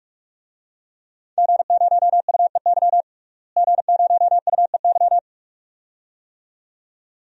Morse code, heard as a keyed tone, about 46 wpm, 700 Hz. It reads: G0UEY G0UEY
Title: DXCC-100 Call Signs - 5 Characters - Repeated 2x - 45wpm